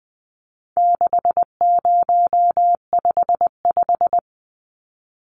Morse code, heard as a keyed tone, about 20 wpm, 700 Hz. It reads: 6055
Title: Numbers - 4-Digit 20wpm